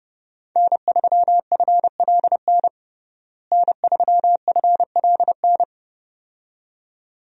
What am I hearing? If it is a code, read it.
Morse code, heard as a keyed tone, about 32 wpm, 700 Hz. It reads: N3FLN N3FLN
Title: USA Call Signs - 5 Characters - Repeated 2x - 30wpm